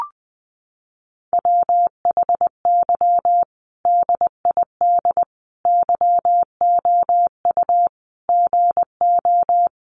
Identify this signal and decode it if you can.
Morse code, heard as a keyed tone, about 20 wpm, 700 Hz. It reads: WHY DID YOU GO
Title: Sentences from Top 1100 Words 20wpm